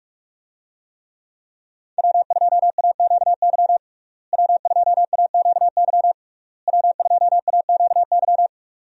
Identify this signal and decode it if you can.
Morse code, heard as a keyed tone, about 47 wpm, 700 Hz. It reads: W2AQY W2AQY W2AQY
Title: USA Call Signs - 5 Characters - Repeated 3x - 45wpm